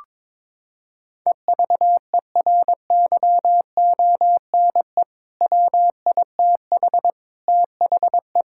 Morse code, heard as a keyed tone, about 22 wpm, 700 Hz. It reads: EVERYONE WITH THE
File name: practice / 22wpm / Sentences with 4-Word Phrases - 22wpm